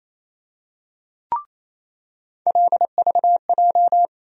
Morse code, heard as a keyed tone, about 28 wpm, 700 Hz. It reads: LVJ
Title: USA Call Sign Postfixes - 3 Letters 28wpm